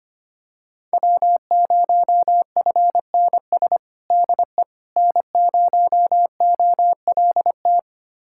Morse code, heard as a keyed tone, about 25 wpm, 700 Hz. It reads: W0FNS DE N0OLT
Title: USA Callsign DE Callsign - 5 Characters - 25wpm